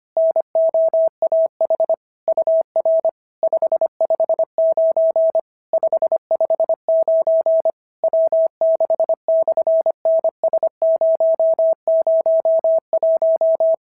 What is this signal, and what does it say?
Morse code, heard as a keyed tone, about 25 wpm, 690 Hz. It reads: NOAH UR 559 559 W6/NS001
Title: SOTA QSO Practice 25wpm